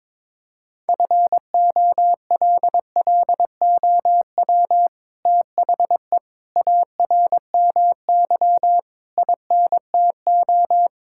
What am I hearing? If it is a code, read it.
Morse code, heard as a keyed tone, about 22 wpm, 700 Hz. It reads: FOLLOW THE ARMY INTO
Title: Sentences from Top 1000 Words - Review Part 5 - 22wpm